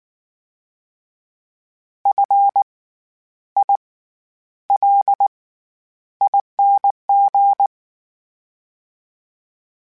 Morse code, heard as a keyed tone, about 19 wpm, spaced as for 10 wpm, 800 Hz. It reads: FILING
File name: Special INGs / Special INGs-mp3s / SlowToFaster ing @18